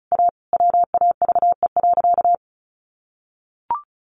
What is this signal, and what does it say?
Morse code, heard as a keyed tone, about 34 wpm, 700 Hz. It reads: A WAVE.
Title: Sentences from Top 1400 Words 35wpm